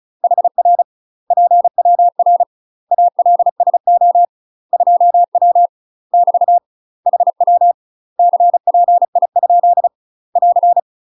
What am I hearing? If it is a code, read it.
Morse code, heard as a keyed tone, about 34 wpm, 700 Hz. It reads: HR PWR ALSO 2W <BT> HW CPI ? <AR>
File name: UK MASTER QSOs TRAINING / 35WPM / UK MASTER QSOs 0034